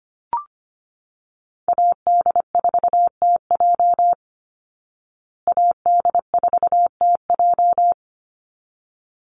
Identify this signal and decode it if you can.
Morse code, heard as a keyed tone, about 25 wpm, 700 Hz. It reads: AD4TJ AD4TJ
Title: Call Signs Repeated 3x - 25wpm